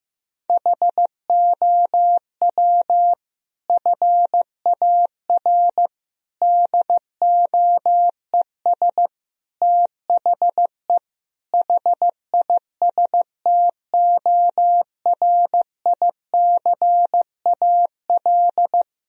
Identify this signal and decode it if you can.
Morse code, heard as a keyed tone, about 15 wpm, 700 Hz. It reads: HOW FAR DOES THE HISTORICAL
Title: Sentences from Top 1000 Words - Review Part 4 - 15wpm